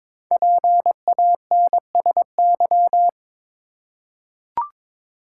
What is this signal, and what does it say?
Morse code, heard as a keyed tone, about 22 wpm, 690 Hz. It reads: PANSY
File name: Difficult Words with Hints - 22wpm